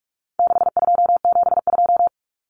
Morse code, heard as a keyed tone, about 44 wpm, 700 Hz. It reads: 6272
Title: Common Cut Numbers - 4-Digit 45wpm